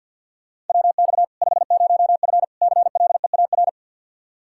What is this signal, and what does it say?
Morse code, heard as a keyed tone, about 48 wpm, 700 Hz. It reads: WX 50F CLEAR
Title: K1USN SST QSO Practice 50wpm